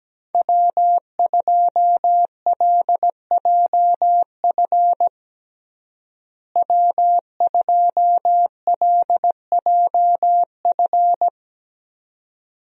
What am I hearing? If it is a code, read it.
Morse code, heard as a keyed tone, about 17 wpm, 700 Hz. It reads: W2LJF W2LJF
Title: USA Call Signs - 5 Characters - Repeated 2x - 17wpm